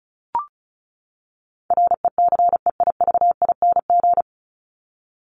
Morse code, heard as a keyed tone, about 34 wpm, 700 Hz. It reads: RECEIVING
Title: ING Suffix - Top 100 Words with Base Words 35wpm